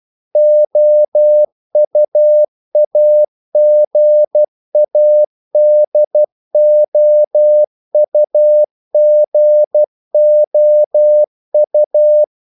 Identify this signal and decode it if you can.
Morse code, heard as a keyed tone, about 12 wpm, 600 Hz. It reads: OUAGADOUGOU
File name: CW / Practice / NationalCapitals12